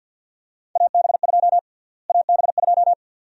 Morse code, heard as a keyed tone, about 53 wpm, 700 Hz. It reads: AB2 AB2
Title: USA Call Sign Prefixes - Repeated 2x - 50wpm